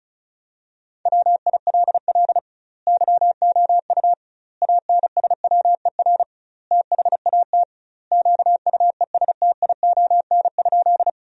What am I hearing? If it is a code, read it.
Morse code, heard as a keyed tone, about 34 wpm, 700 Hz. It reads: WILL YOU ANSWER THAT QUESTION?
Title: 2x Word-Spacing - Sentences from Top 400 Words - 35wpm